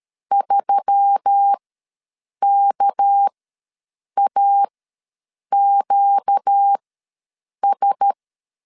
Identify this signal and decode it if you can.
Morse code, heard as a keyed tone, about 13 wpm, spaced as for 7 wpm, 800 Hz. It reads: 3KAQS
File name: LEZIONI / 6 PKX0 / 6 PK ws 12wpm 3116